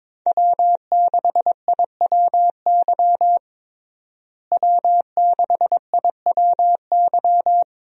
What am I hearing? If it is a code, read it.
Morse code, heard as a keyed tone, about 22 wpm, 700 Hz. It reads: W6IWY W6IWY
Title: USA Call Signs - 5 Characters - Repeated 2x - 22wpm